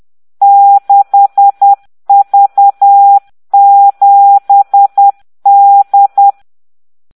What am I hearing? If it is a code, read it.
Morse code, heard as a keyed tone, about 10 wpm, 800 Hz. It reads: 6V7D